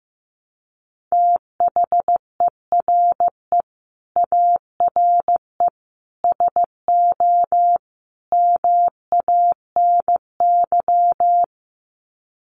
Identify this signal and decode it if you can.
Morse code, heard as a keyed tone, about 15 wpm, 700 Hz. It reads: THERE ARE SO MANY
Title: Sets of 4 Words 15wpm